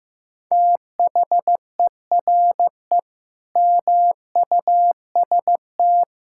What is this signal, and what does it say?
Morse code, heard as a keyed tone, about 15 wpm, 700 Hz. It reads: THERE MUST
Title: Sentences from Top 1000 Words - Review Part 6 - 15wpm